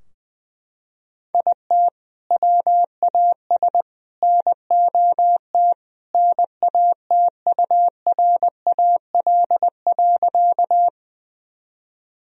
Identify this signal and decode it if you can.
Morse code, heard as a keyed tone, about 20 wpm, 700 Hz. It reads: IT WAS NOT NATURAL.
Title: Sentences from Top 1000 Words - Review Part 4 - 20wpm